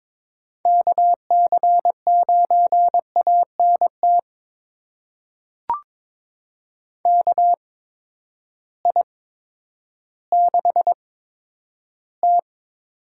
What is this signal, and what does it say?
Morse code, heard as a keyed tone, about 23 wpm, spaced as for 11 wpm, 700 Hz. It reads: KC9ANT KI6T
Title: USA Call Signs - 8x Character Spacing - 22wpm